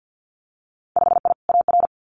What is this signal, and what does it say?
Morse code, heard as a keyed tone, about 50 wpm, 700 Hz. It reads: 5I AR
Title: Winter Field Day Exchanges Encore 50wpm